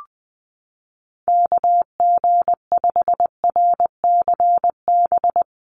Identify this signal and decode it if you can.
Morse code, heard as a keyed tone, about 20 wpm, 700 Hz. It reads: KG5RCB
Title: USA Call Signs 20wpm